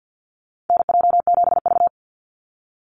Morse code, heard as a keyed tone, about 50 wpm, 700 Hz. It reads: NJ7V
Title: NA SOTA Chaser Callsigns 50wpm